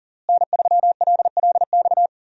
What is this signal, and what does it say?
Morse code, heard as a keyed tone, about 40 wpm, 700 Hz. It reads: N3LLX
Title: USA Call Signs - 5 Characters - Repeated 3x - 40wpm